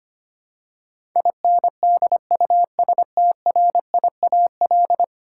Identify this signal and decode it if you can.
Morse code, heard as a keyed tone, about 25 wpm, 700 Hz. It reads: INDUSTRIAL
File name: Ten Letter Words - 25wpm